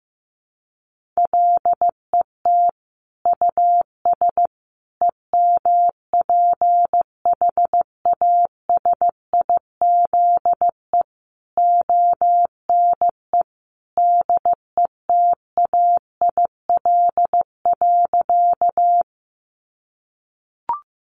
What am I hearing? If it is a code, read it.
Morse code, heard as a keyed tone, about 15 wpm, 700 Hz. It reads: LET US EMPHASIZE ONE DETAIL.
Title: Sentences from Top 1500 Words 15wpm